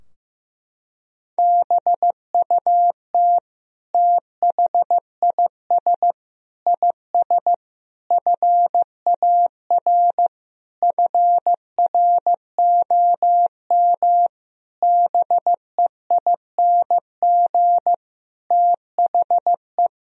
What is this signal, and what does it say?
Morse code, heard as a keyed tone, about 16 wpm, 700 Hz. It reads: BUT THIS IS FAR FROM BEING THE
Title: Sentences from Top 1000 Words - Review Part 3 - 15wpm